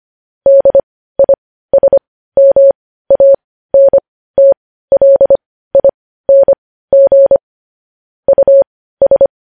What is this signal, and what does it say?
Morse code, heard as a keyed tone, about 25 wpm, spaced as for 15 wpm, 550 Hz. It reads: DISMANTLING US
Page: news-headlines-in-morse-code-at-15-wpm